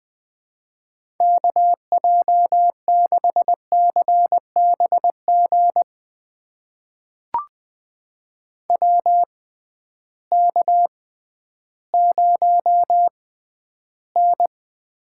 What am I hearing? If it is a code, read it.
Morse code, heard as a keyed tone, about 20 wpm, spaced as for 12 wpm, 700 Hz. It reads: KJ6CBG WK0N
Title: USA Call Signs - 6x Character Spacing - 20wpm